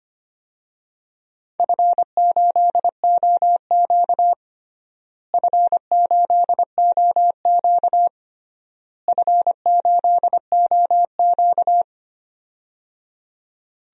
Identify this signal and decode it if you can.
Morse code, heard as a keyed tone, about 25 wpm, 700 Hz. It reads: F8OQ F8OQ F8OQ
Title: DXCC-100 Call Signs - 4 Characters - Repeated 3x - 25wpm